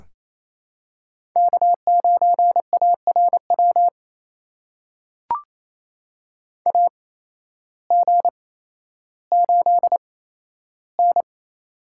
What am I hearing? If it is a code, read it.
Morse code, heard as a keyed tone, about 28 wpm, spaced as for 13 wpm, 700 Hz. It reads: K9ARW AG8N
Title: USA Call Signs - 8x Character Spacing - 28wpm